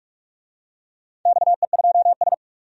Morse code, heard as a keyed tone, about 47 wpm, 700 Hz. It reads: XE2S
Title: DXCC-100 Call Signs - 4 Characters - 45wpm